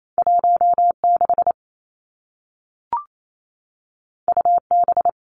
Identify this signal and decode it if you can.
Morse code, heard as a keyed tone, about 28 wpm, 700 Hz. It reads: 1 6 U B
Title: Two Challenging Letter-Number Combinations 28wpm